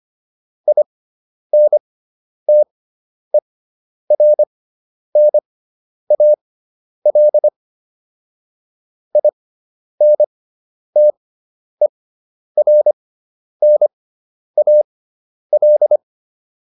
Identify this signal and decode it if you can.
Morse code, heard as a keyed tone, about 25 wpm, spaced as for 10 wpm, 600 Hz. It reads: INTERNAL INTERNAL